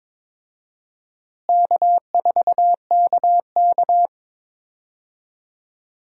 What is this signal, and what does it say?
Morse code, heard as a keyed tone, about 21 wpm, 700 Hz. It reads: K4KK
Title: USA Call Signs - 4 Characters - Repeated 3x - 22wpm